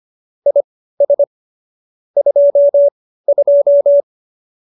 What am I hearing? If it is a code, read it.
Morse code, heard as a keyed tone, about 25 wpm, spaced as for 15 wpm, 570 Hz. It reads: IS 22